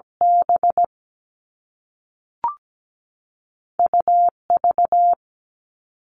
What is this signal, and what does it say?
Morse code, heard as a keyed tone, about 17 wpm, 700 Hz. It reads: B UV
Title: All Cut Numbers - 2-Digit 17wpm